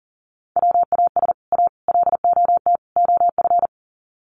Morse code, heard as a keyed tone, about 40 wpm, 700 Hz. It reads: WAS A LOT OF
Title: Sets of 4 Words 40wpm